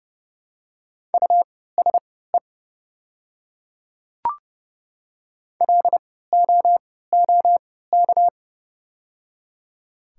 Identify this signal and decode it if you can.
Morse code, heard as a keyed tone, about 30 wpm, spaced as for 17 wpm, 700 Hz. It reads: USE LOOK
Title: 3x Character Spacing - Top 200 Words - 30wpm